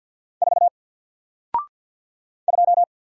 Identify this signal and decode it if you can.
Morse code, heard as a keyed tone, about 48 wpm, 700 Hz. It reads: 4 2
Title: Single Character (Letter:Number:Punctuation:Prosign) 50wpm